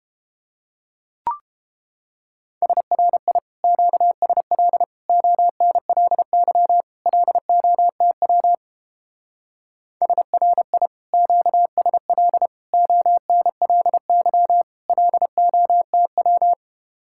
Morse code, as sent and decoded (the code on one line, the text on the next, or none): ... .-. .. / --.- ... .-.. / --- -. .-.. -.-- / .-.. --- - .-- / ... .-. .. / --.- ... .-.. / --- -. .-.. -.-- / .-.. --- - .--
SRI QSL ONLY LOTW SRI QSL ONLY LOTW